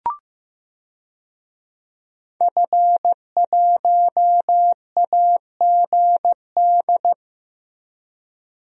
Morse code, heard as a keyed tone, about 15 wpm, 700 Hz. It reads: F1AGD
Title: DXCC-100 Call Signs - 5 Characters - Repeated 3x - 15wpm